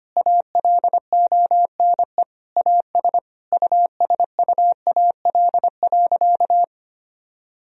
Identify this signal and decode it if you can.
Morse code, heard as a keyed tone, about 25 wpm, 700 Hz. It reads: ALONE AS USUAL.
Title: Sentences from Top 1000 Words - Review Part 2 - 25wpm